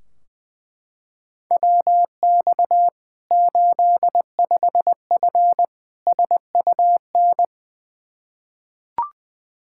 Morse code, heard as a keyed tone, about 20 wpm, 700 Hz. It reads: WX 85F SUN
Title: Practice QSO - Exchange 2 - Info - 20wpm